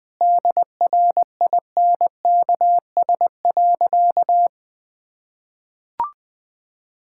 Morse code, heard as a keyed tone, about 20 wpm, 700 Hz. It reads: DRINKS.
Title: Sentences with 6-Word Phrases - Spoken Beginning - 20wpm